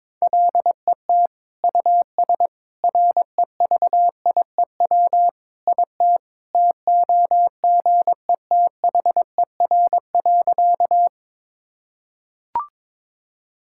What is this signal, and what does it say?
Morse code, heard as a keyed tone, about 22 wpm, 700 Hz. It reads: LET US REVIEW IT TOGETHER.
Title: Sentences from Top 1000 Words - Review Part 6 - 22wpm